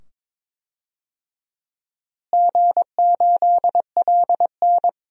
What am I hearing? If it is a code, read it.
Morse code, heard as a keyed tone, about 22 wpm, 700 Hz. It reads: G8LN
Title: DXCC-100 Call Signs - 4 Characters - Repeated 2x - 22wpm